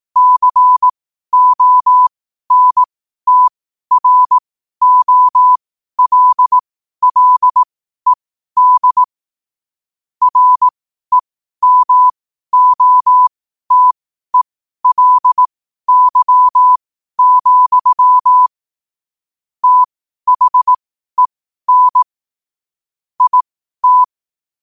Morse code, heard as a keyed tone, about 18 wpm, spaced as for 12 wpm, 1000 Hz. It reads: CONTROLLED REMOTELY, THEN IT
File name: qotdfiles / 12wpm